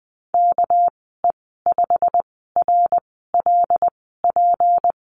Morse code, heard as a keyed tone, about 20 wpm, spaced as for 15 wpm, 700 Hz. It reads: KE5RLP
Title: 219 Lesson 35g - Number 0 - Call Signs 20wpm